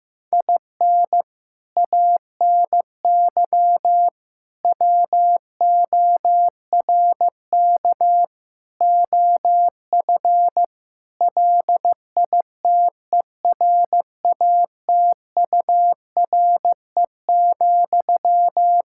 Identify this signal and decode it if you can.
Morse code, heard as a keyed tone, about 15 wpm, 700 Hz. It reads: IN ANY WORK OF LITERATURE,
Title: Sentences from Top 1000 Words - Review Part 6 - 15wpm